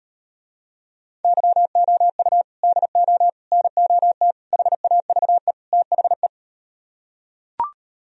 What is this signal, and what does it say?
Morse code, heard as a keyed tone, about 38 wpm, 700 Hz. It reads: YOU DO NOT HAVE THE